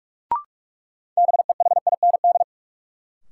Morse code, heard as a keyed tone, about 47 wpm, 700 Hz. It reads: BEHIND
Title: Sentences from Top 500 Words - New Word Drill - 45wpm